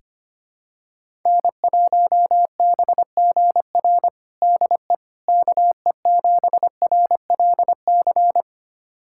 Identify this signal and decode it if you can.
Morse code, heard as a keyed tone, about 25 wpm, 700 Hz. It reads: N1BGR DE KE7RLC
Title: USA Callsign DE Callsign 25wpm